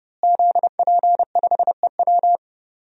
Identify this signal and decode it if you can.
Morse code, heard as a keyed tone, about 30 wpm, 700 Hz. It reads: ZP5EW
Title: DXCC-100 Call Signs - 5 Characters - Repeated 2x - 30wpm